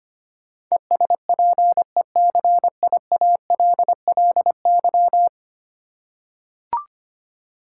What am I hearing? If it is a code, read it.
Morse code, heard as a keyed tone, about 25 wpm, 700 Hz. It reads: ESPECIALLY
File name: Ten Letter Words - 25wpm